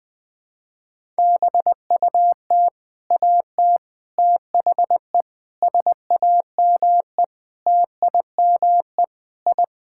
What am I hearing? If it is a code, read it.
Morse code, heard as a keyed tone, about 20 wpm, 700 Hz. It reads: BUT AT THE SAME TIME I
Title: Sentences with 4-Word Phrases - 20wpm